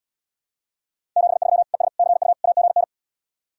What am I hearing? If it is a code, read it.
Morse code, heard as a keyed tone, about 80 wpm, 700 Hz. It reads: 73 ES CU AGN